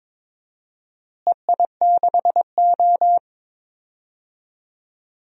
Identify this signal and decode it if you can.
Morse code, heard as a keyed tone, about 22 wpm, 700 Hz. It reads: EI6O
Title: DXCC-100 Call Signs - 4 Characters - Repeated 2x - 22wpm